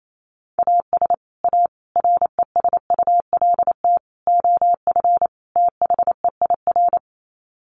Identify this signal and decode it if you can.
Morse code, heard as a keyed tone, about 28 wpm, 700 Hz. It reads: AS A RESULT OF THEIR
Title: Sets of 5 Words Encore2 28wpm